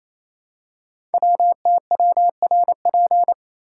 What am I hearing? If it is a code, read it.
Morse code, heard as a keyed tone, about 27 wpm, 700 Hz. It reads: WTWRP
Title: Code Groups - 5 Letters - 28wpm